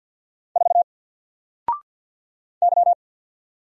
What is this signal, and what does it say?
Morse code, heard as a keyed tone, about 50 wpm, 700 Hz. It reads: 4 Y
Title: Single Letter-Number 50wpm